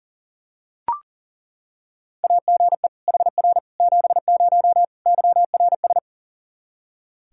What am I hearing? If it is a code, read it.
Morse code, heard as a keyed tone, about 38 wpm, 700 Hz. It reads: AGE HR 70 YRS
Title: Practice QSO - Exchange 3 - More Info - 40wpm